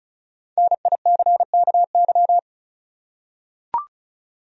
Morse code, heard as a keyed tone, about 36 wpm, 700 Hz. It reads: NICKY
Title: USA Common Names - Max 5 Letters - 35wpm